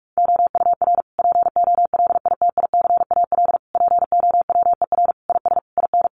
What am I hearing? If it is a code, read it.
Morse code, heard as a keyed tone, about 44 wpm, 700 Hz. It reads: OUR POLITICAL POWER IS IN
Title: Sentences from Top 900 Words 45wpm